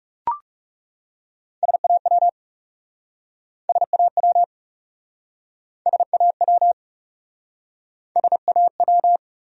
Morse code, heard as a keyed tone, about 40 wpm, 700 Hz. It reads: SAW SAW SAW SAW